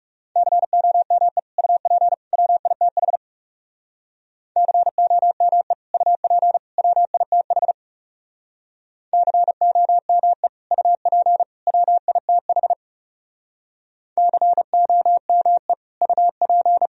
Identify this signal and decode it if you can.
Morse code, heard as a keyed tone, about 36 wpm, 700 Hz. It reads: COME UP WITH COME UP WITH COME UP WITH COME UP
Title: Speed-Racing - Sets of 3 Words 30wpm